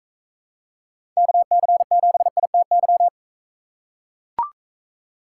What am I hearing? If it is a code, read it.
Morse code, heard as a keyed tone, about 40 wpm, 700 Hz. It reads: KC7ITY